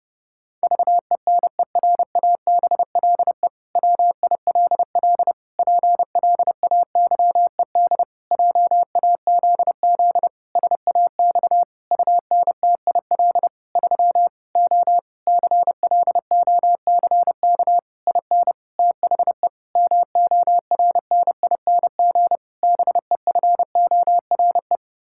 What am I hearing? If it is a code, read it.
Morse code, heard as a keyed tone, about 28 wpm, 700 Hz. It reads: VENERABLE WILL PLAYED JAZZ SAX UNTIL 3 O CLOCK IN THE MORNING BEFORE
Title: Pangram Challenge v2 - 30wpm